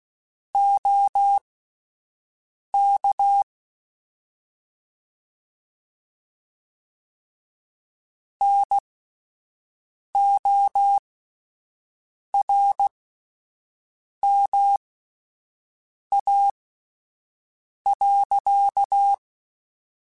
Morse code, heard as a keyed tone, about 16 wpm, spaced as for 5 wpm, 800 Hz. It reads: OK NORMA.
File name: (spaced) / QSOs Spaced-mp3s / WD0FBT 1 @10 sp